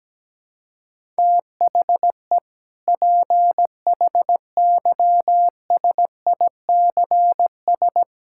Morse code, heard as a keyed tone, about 17 wpm, 700 Hz. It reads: THE PHYSICS
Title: Sentences from Top 1000 Words - Review Part 5 - 17wpm